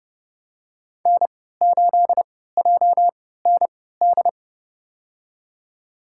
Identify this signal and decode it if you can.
Morse code, heard as a keyed tone, about 30 wpm, spaced as for 17 wpm, 700 Hz. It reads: N8JND
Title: USA Call Signs - 5 Characters - 3x Character Spacing - 30wpm